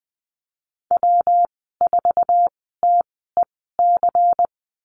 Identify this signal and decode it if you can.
Morse code, heard as a keyed tone, about 20 wpm, spaced as for 14 wpm, 700 Hz. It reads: W4TEC
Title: 128 Lesson 21g - Number 6 - Call Signs 20wpm